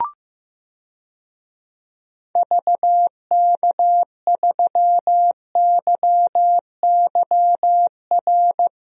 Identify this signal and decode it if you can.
Morse code, heard as a keyed tone, about 15 wpm, 700 Hz. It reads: VK3YYR
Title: DXCC-100 Call Signs - 6 Characters - Repeated 2x - 15wpm